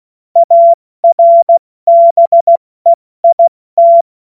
Morse code, heard as a keyed tone, about 16 wpm, 680 Hz. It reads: ARBEIT
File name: Morsekurs / Witze / Witz 21.November 2023 70 F80